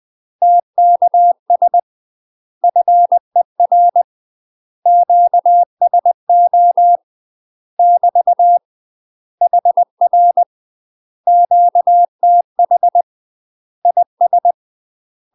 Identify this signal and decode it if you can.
Morse code, heard as a keyed tone, about 20 wpm, 700 Hz. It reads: TKS FER QSO <BT> HR QTH IS
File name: UK MASTER QSOs TRAINING / 20WPM / UK MASTER QSOs 0049